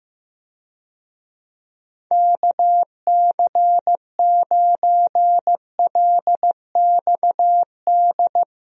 Morse code, heard as a keyed tone, about 15 wpm, 700 Hz. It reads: KC9LXD
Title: USA Call Signs - 6 Characters - Repeated 3x - 15wpm